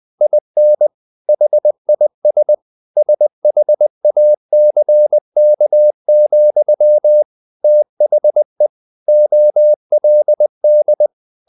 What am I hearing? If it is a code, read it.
Morse code, heard as a keyed tone, about 20 wpm, 600 Hz. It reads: IN HIS SHACK, THE OLD
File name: the old man and the sea part 2 20wpm